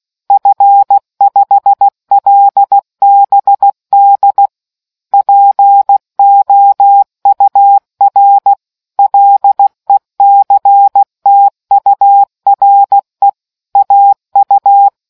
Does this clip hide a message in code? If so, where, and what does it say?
Yes: Morse code from 0.3 to 14.9 s, heard as a keyed tone, about 16 wpm, 800 Hz. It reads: F5LBD POUR LECTURE AU